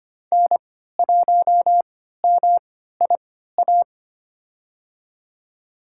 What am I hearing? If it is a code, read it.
Morse code, heard as a keyed tone, about 25 wpm, spaced as for 14 wpm, 700 Hz. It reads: N1MIA
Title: USA Call Signs - 5 Characters - 3x Character Spacing - 25wpm